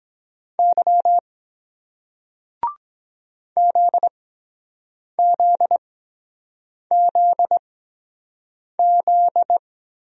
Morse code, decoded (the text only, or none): Y ZZZZ